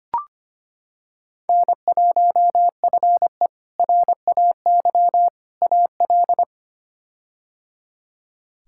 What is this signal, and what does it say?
Morse code, heard as a keyed tone, about 25 wpm, 700 Hz. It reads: N1FE RAY AL
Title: K1USN SST QSO Practice - 25wpm